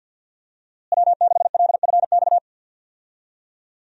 Morse code, heard as a keyed tone, about 52 wpm, 700 Hz. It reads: W6LFX
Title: USA Call Signs - 5 Characters - Repeated 2x - 50wpm